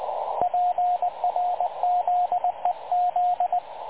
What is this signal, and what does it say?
Morse code, heard as a keyed tone, about 19 wpm, 720 Hz, with radio noise. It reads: PRZEZ
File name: dlazaawansowanych / nagraniazpasma / websdr recording 3567.0kHz